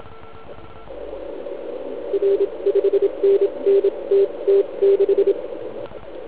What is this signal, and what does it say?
Morse code, heard as a keyed tone, about 27 wpm, spaced as for 23 wpm, 400 Hz. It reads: R 5NN TT6
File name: Archiv / CUC149 / OK1IF CUC149